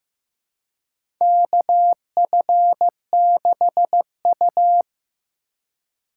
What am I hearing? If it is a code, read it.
Morse code, heard as a keyed tone, about 15 wpm, 700 Hz. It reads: KF6U
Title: USA Call Signs - 4 Characters - 4x Character Spacing - 15wpm